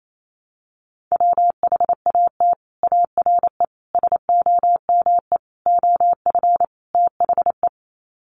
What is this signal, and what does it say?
Morse code, heard as a keyed tone, about 28 wpm, 700 Hz. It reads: WHAT ARE SOME OF THE
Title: Sets of 5 Words Encore 28wpm